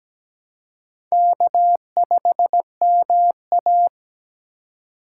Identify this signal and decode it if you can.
Morse code, heard as a keyed tone, about 17 wpm, 700 Hz. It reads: K5MA
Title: K1USN SST QSO Practice - 17wpm